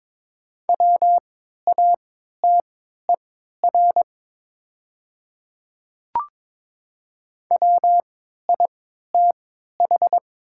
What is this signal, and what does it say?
Morse code, heard as a keyed tone, about 22 wpm, spaced as for 12 wpm, 700 Hz. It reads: WATER WITH
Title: 3x Character Spacing - Top 200 Words - 22wpm